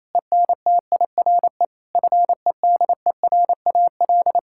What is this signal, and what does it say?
Morse code, heard as a keyed tone, about 28 wpm, 700 Hz. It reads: ENTIRE FEDERAL
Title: Sentences from Top 1000 Words - Review Part 5 - 28wpm